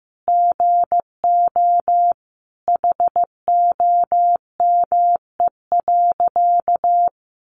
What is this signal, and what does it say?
Morse code, heard as a keyed tone, about 15 wpm, 700 Hz. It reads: GO HOME.
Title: Sentences from Top 1400 Words 15wpm